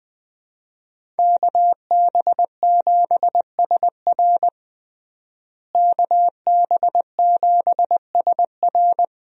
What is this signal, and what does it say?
Morse code, heard as a keyed tone, about 20 wpm, 700 Hz. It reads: KB7SR KB7SR
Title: USA Call Signs - 5 Characters - Repeated 3x - 20wpm